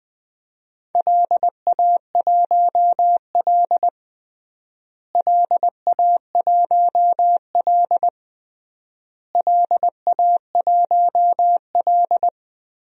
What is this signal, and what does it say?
Morse code, heard as a keyed tone, about 20 wpm, 700 Hz. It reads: LA1L LA1L LA1L
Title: DXCC-100 Call Signs - 4 Characters - Repeated 3x - 20wpm